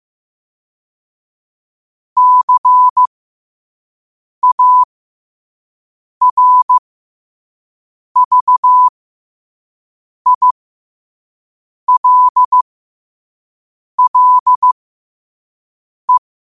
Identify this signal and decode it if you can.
Morse code, heard as a keyed tone, about 15 wpm, spaced as for 5 wpm, 1000 Hz. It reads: CARVILLE
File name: qotdfiles / 5wpm